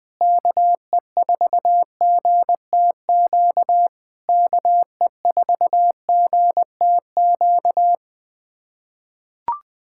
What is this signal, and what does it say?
Morse code, heard as a keyed tone, about 20 wpm, 700 Hz. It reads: KE4GTQ KE4GTQ
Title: POTA QSO Practice - 20wpm